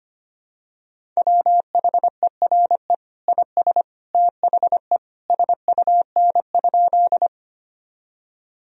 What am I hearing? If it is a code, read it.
Morse code, heard as a keyed tone, about 25 wpm, 700 Hz. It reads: WHERE IS THE SUN?
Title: Sentences from Top 1000 Words - Review Part 1 - 25wpm